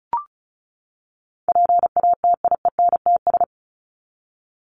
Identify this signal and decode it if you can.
Morse code, heard as a keyed tone, about 34 wpm, 700 Hz. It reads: PATIENTS
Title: ES:S Suffix - Top 100 Words with Base Words 35wpm